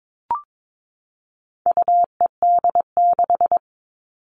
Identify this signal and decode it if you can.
Morse code, heard as a keyed tone, about 22 wpm, 700 Hz. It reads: UED6
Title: All Cut Numbers - 4-Digit 22wpm